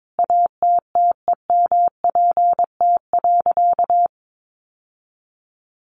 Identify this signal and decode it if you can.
Morse code, heard as a keyed tone, about 22 wpm, 700 Hz. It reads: ATTEMPT.
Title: Sentences from Top 1000 Words 22wpm